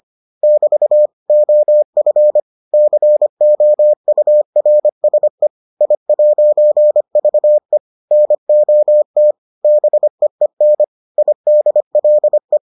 Morse code, heard as a keyed tone, about 25 wpm, 600 Hz. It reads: <BT> OF COURSE I'VE NOT BEEN IDLE